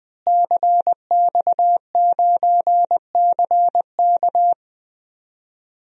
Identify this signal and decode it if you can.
Morse code, heard as a keyed tone, about 20 wpm, 700 Hz. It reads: CX9CK
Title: DXCC-100 Call Signs - 5 Characters - Repeated 2x - 20wpm